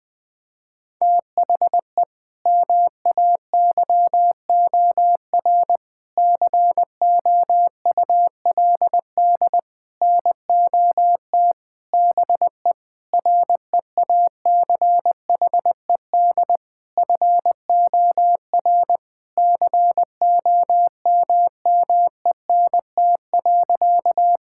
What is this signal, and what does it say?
Morse code, heard as a keyed tone, about 20 wpm, 700 Hz. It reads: THE MAYOR COULD NOT BE REACHED FOR COMMENT.
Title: Sentences with 6-Word Phrases - Spoken Beginning - 20wpm